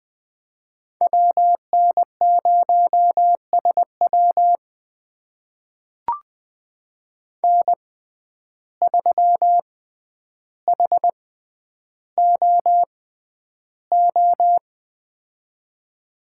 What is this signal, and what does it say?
Morse code, heard as a keyed tone, about 20 wpm, spaced as for 10 wpm, 700 Hz. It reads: WN0SW N3HOO
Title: USA Call Signs - 5 Characters - 6x Character Spacing - 20wpm